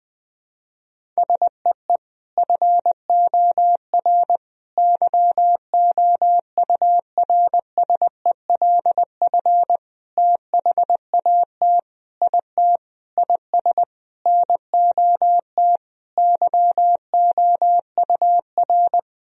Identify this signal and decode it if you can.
Morse code, heard as a keyed tone, about 20 wpm, 700 Hz. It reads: SEE FOR YOURSELF THAT IT IS NOT YOUR
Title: Sentences from Top 1000 Words - Review Part 1 - 20wpm